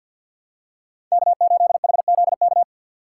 Morse code, heard as a keyed tone, about 46 wpm, 690 Hz. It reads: K8HZK
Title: USA Call Signs - 5 Characters - 2x Character Spacing - 50wpm